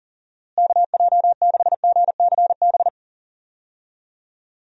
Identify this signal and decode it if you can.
Morse code, heard as a keyed tone, about 40 wpm, 700 Hz. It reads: KJ6GCB
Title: USA Call Signs - 6 Characters - Repeated 2x - 40wpm